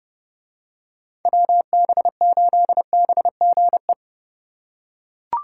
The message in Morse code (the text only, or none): WB8BGE